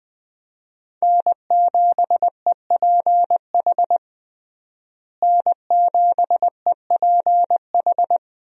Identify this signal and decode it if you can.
Morse code, heard as a keyed tone, about 20 wpm, 700 Hz. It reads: N7EPH N7EPH
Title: USA Call Signs - 5 Characters - Repeated 2x - 20wpm